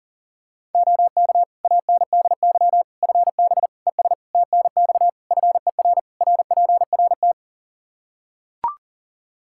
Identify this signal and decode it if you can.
Morse code, heard as a keyed tone, about 37 wpm, 700 Hz. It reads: OK ANDY FB ES TNX FER RPRT
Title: K1USN SST QSO Practice 40wpm